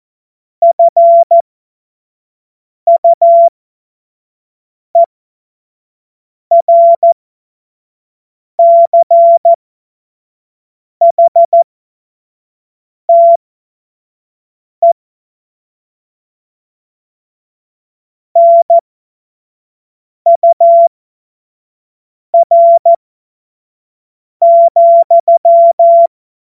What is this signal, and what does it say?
Morse code, heard as a keyed tone, about 14 wpm, spaced as for 5 wpm, 680 Hz. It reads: FUERCHTE NUR,
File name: Witz-08-Juli-2025-25-F70